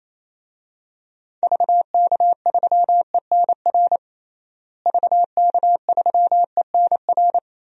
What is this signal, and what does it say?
Morse code, heard as a keyed tone, about 28 wpm, 700 Hz. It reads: VK3ENR VK3ENR
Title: DXCC-100 Call Signs - 6 Characters - Repeated 3x - 28wpm